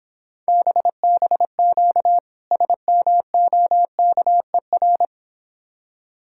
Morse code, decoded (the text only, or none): BBQ SMOKER